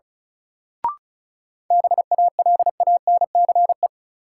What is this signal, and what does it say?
Morse code, heard as a keyed tone, about 34 wpm, 700 Hz. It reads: BALANCE
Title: Sentences from Top 900 Words - New Word Drill - 35wpm